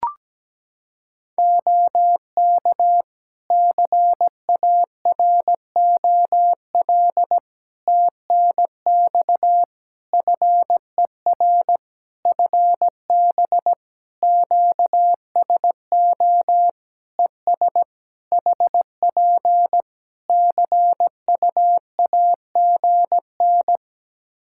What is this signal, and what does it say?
Morse code, heard as a keyed tone, about 17 wpm, 700 Hz. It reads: OK CAROL TNX FER FB QSO ES HP CUAGN
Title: Practice QSO - Exchange 4 - Ending - 17wpm